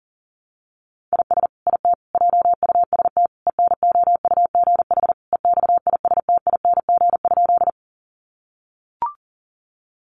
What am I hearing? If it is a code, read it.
Morse code, heard as a keyed tone, about 40 wpm, 700 Hz. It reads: IS IT JUST ENOUGH EXISTING?
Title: Sentences from Top 1500 Words 40wpm